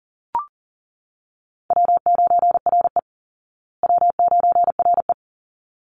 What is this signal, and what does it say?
Morse code, heard as a keyed tone, about 40 wpm, 700 Hz. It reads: W9RE W9RE
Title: Easy Call Signs Repeated 2x - 40wpm